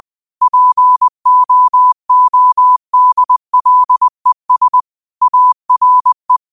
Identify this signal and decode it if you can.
Morse code, heard as a keyed tone, about 20 wpm, 1000 Hz. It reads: POODLES ARE
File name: qotdfiles / 20wpm